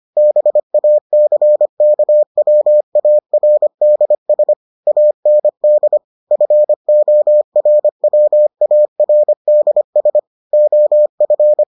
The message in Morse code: BACKWARDS AND FORWARDS OF

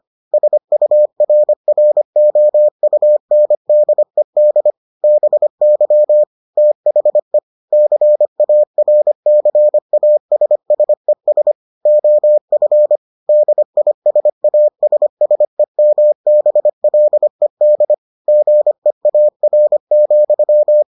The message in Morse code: SURROUNDED BY THE CARCASSES OF DISASSEMBLED GEAR,

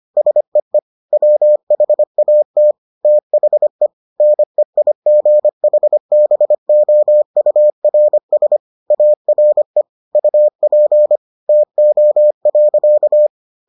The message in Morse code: SEE WHAT THE NEIGHBOURS ARE UP TO.

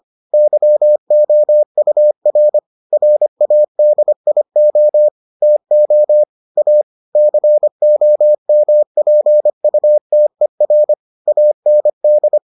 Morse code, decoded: YOUR RADIO TO A COMPUTER AND

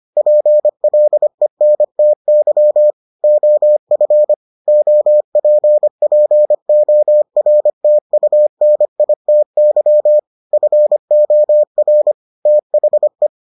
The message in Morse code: PLENTY OF OPPORTUNITY FOR THE

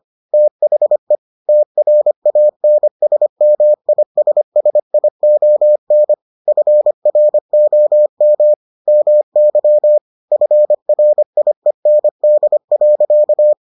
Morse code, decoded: THE TRANSMISSION FROM MY FRIEND.

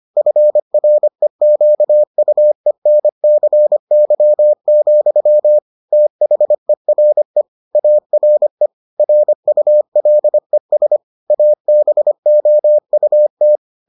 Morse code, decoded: FREQUENCY, THERE ARE RULES ABOUT

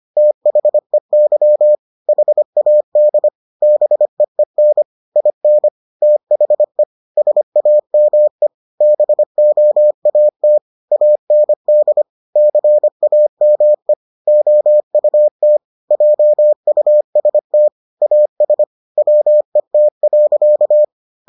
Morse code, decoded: THEY HAD BEEN IN THE SAME BOAT AND CAME OUT JUST AS WET.

